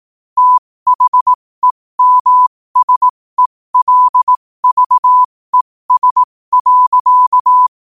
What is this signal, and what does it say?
Morse code, heard as a keyed tone, about 18 wpm, spaced as for 15 wpm, 1000 Hz. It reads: THEMSELVES.
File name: qotdfiles / 15wpm